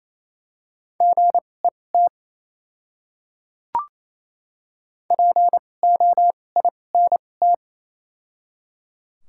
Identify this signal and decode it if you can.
Morse code, heard as a keyed tone, about 28 wpm, spaced as for 20 wpm, 700 Hz. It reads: GET POINT
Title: 2x Character Spacing - Top 200 Words - 28wpm